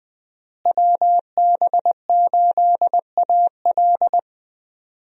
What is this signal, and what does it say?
Morse code, heard as a keyed tone, about 20 wpm, 700 Hz. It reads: WB8AL
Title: USA Call Signs - 5 Characters - Repeated 2x - 20wpm